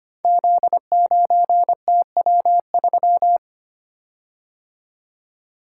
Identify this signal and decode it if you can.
Morse code, heard as a keyed tone, about 25 wpm, 700 Hz. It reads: Z9TW3
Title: Code Groups - 5 Letter:Numbers - 25wpm